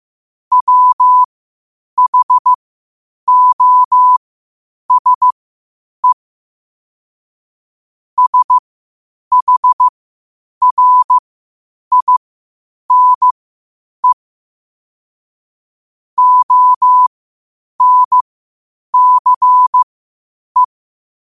Morse code, heard as a keyed tone, about 15 wpm, spaced as for 8 wpm, 1000 Hz. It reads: WHOSE SHRINE ONCE
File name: qotdfiles / 8wpm